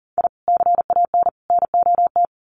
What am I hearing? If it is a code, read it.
Morse code, heard as a keyed tone, about 40 wpm, 700 Hz. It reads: I CAN NOT
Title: Sets of 3 Words 40wpm